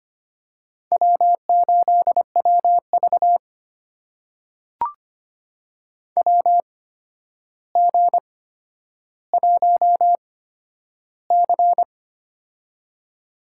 Morse code, heard as a keyed tone, about 25 wpm, spaced as for 11 wpm, 700 Hz. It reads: W8WV WG1C
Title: USA Call Signs - 4 Characters - 8x Character Spacing - 25wpm